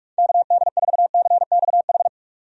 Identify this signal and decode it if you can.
Morse code, heard as a keyed tone, about 46 wpm, 700 Hz. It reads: KD4CXH
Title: USA Call Signs - 6 Characters - 45wpm